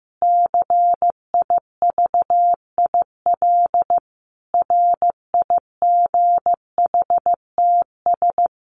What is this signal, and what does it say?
Morse code, heard as a keyed tone, about 15 wpm, 700 Hz. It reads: CIVIL RIGHTS
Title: Sets of 5 Words Encore2 15wpm